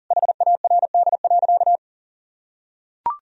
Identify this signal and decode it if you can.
Morse code, heard as a keyed tone, about 40 wpm, 700 Hz. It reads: HARD.
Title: Sentences from Top 1000 Words - Review Part 1 - 40wpm